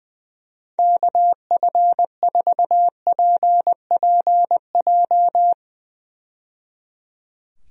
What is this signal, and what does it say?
Morse code, heard as a keyed tone, about 20 wpm, 700 Hz. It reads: KF4PPJ
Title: USA Call Signs - 6 Characters - 20wpm